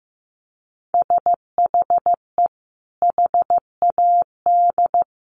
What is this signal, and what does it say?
Morse code, heard as a keyed tone, about 15 wpm, 700 Hz. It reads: SHE HAD
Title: Sentences from Top 1500 Words 15wpm